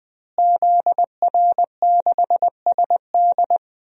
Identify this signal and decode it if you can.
Morse code, heard as a keyed tone, about 20 wpm, 700 Hz. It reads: ZR6SD
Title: DXCC-100 Call Signs - 5 Characters - Repeated 3x - 20wpm